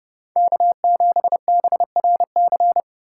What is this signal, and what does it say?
Morse code, heard as a keyed tone, about 31 wpm, 700 Hz. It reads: K7BRC
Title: USA Call Signs - 5 Characters - 3x Character Spacing - 30wpm